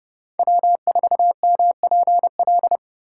Morse code, heard as a keyed tone, about 28 wpm, 700 Hz. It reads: W4MPL